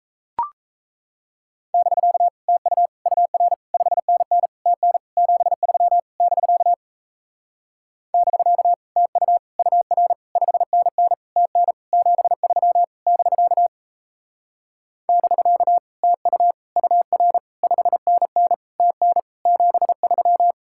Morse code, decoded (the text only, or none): <BK> TU UR 5NN TN 73 <BK> <BK> TU UR 5NN TN 73 <BK> <BK> TU UR 5NN TN 73